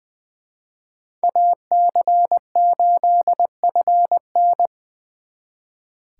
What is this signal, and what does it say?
Morse code, heard as a keyed tone, about 24 wpm, 700 Hz. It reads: AC8FN
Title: Speed Racing - USA 2x2 Call Signs 20wpm